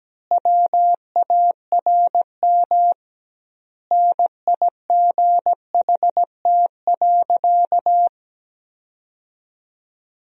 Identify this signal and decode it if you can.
Morse code, heard as a keyed tone, about 17 wpm, 700 Hz. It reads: WARM NIGHT.
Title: 2x Word-Spacing - Sentences from Top 500 Words - 17wpm